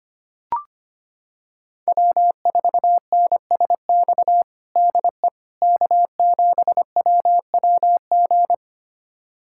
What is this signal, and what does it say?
Morse code, heard as a keyed tone, about 25 wpm, 700 Hz. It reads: W4NSX DE K7WWG
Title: USA Callsign DE Callsign 25wpm